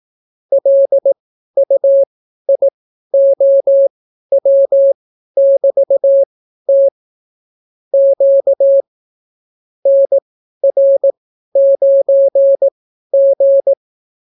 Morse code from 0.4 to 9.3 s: LUIOW<BT>T Q